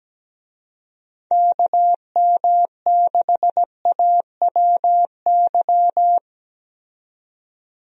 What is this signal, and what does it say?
Morse code, heard as a keyed tone, about 17 wpm, 700 Hz. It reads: KM6AWY
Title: USA Call Signs - 6 Characters - 17wpm